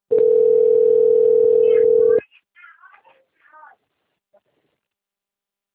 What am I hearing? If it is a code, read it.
Ringback tone (North America), 440 + 480 Hz.